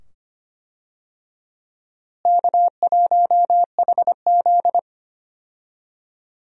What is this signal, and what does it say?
Morse code, heard as a keyed tone, about 25 wpm, 700 Hz. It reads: K1HZ
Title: USA Call Signs - 4 Characters - Repeated 2x - 25wpm